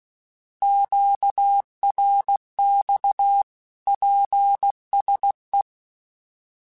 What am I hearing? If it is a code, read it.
Morse code, heard as a keyed tone, about 16 wpm, 800 Hz. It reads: QRX PSE
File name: QSOs / QSOs-mp3s / kc7sac@15